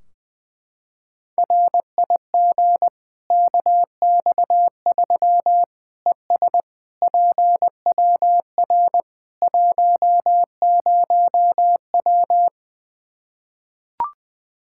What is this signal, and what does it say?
Morse code, heard as a keyed tone, about 20 wpm, 700 Hz. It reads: RIG KX3 ES PWR 10W
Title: Practice QSO - Exchange 2 - Info - 20wpm